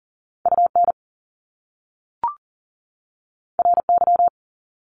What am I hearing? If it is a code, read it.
Morse code, heard as a keyed tone, about 38 wpm, 700 Hz. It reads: UN RY
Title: Top Two Letter Combinations in Words 40wpm